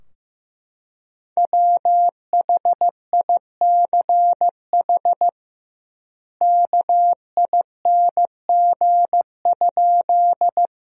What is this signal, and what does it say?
Morse code, heard as a keyed tone, about 15 wpm, 700 Hz. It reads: WHICH KING?
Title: 2x Word-Spacing - Sentences from Top 400 Words - 15wpm